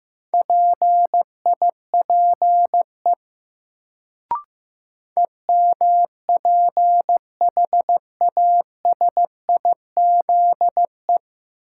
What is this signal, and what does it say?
Morse code, heard as a keyed tone, about 15 wpm, 700 Hz. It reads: PIPE EMPHASIZE
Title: Sentences from Top 1500 Words - New Word Drill Alt - 15wpm